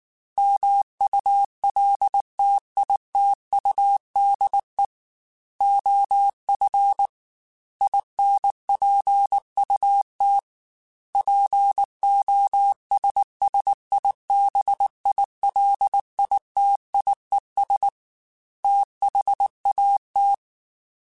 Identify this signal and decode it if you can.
Morse code, heard as a keyed tone, about 19 wpm, 800 Hz. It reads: MULTITUDE OF INPUT POSSIBILITIES THAT
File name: Technical Subjects / Technical Subjects-mp3s / OR Gate @18